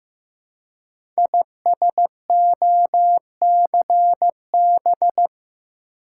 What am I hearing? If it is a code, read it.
Morse code, heard as a keyed tone, about 15 wpm, 700 Hz. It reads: ISOCB
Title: Code Groups - 5 Letters - 15wpm